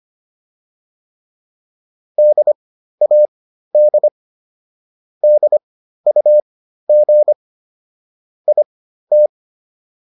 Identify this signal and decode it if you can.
Morse code, heard as a keyed tone, about 25 wpm, spaced as for 13 wpm, 600 Hz. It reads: DAD DUG IT